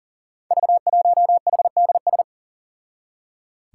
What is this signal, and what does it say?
Morse code, heard as a keyed tone, about 40 wpm, 700 Hz. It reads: V1HDS
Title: Five Challenging Letter-Number Combinations 40wpm